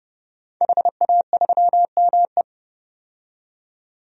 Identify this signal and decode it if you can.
Morse code, heard as a keyed tone, about 32 wpm, 700 Hz. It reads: HA3ME
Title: DXCC-100 Call Signs - 5 Characters - 30wpm